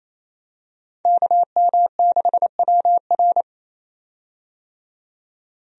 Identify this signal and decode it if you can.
Morse code, heard as a keyed tone, about 28 wpm, 700 Hz. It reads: KM6WR
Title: USA Call Signs - 5 Characters - 28wpm